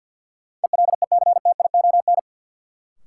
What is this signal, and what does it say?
Morse code, heard as a keyed tone, about 53 wpm, 700 Hz. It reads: ELECTION
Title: Sentences from Top 1500 Words - New Word Drill Alt - 50wpm